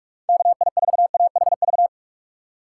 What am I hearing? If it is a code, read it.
Morse code, heard as a keyed tone, about 41 wpm, 700 Hz. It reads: KI4AHV
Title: USA Call Signs - 6 Characters - 3x Character Spacing - 45wpm